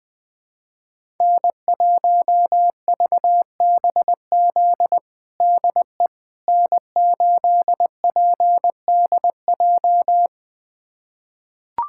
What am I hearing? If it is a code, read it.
Morse code, heard as a keyed tone, about 20 wpm, 700 Hz. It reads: N1VBZ DE N8PDJ
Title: USA Callsign DE Callsign - 5 Characters - 20wpm